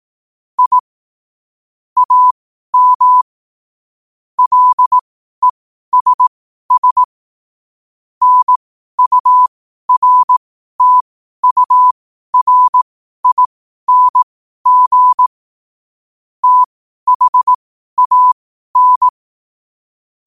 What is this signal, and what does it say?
Morse code, heard as a keyed tone, about 18 wpm, spaced as for 12 wpm, 1000 Hz. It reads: I AM LESS NURTURING THAN
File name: qotdfiles / 12wpm